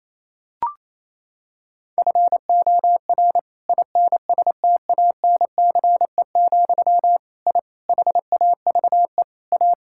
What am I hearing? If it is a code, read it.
Morse code, heard as a keyed tone, about 28 wpm, 700 Hz. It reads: FOR INSTANCE, I HAVE A
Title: Sentences from Top 1000 Words - Review Part 5 - 28wpm